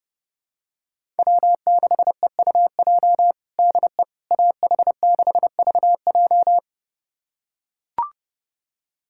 Morse code, heard as a keyed tone, about 30 wpm, 700 Hz. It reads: W6EUJ DE AH6VJ
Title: USA Callsign DE Callsign - 5 Characters - 30wpm